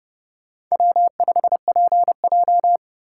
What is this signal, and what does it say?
Morse code, heard as a keyed tone, about 37 wpm, 700 Hz. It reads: W5PJ
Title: Speed Racing - USA 1x2 Call Signs 30wpm